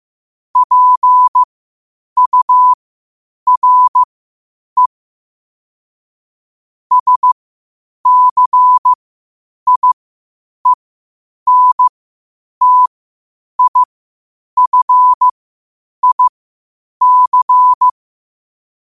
Morse code, heard as a keyed tone, about 15 wpm, spaced as for 8 wpm, 1000 Hz. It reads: PURE SCIENTIFIC
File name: qotdfiles / 8wpm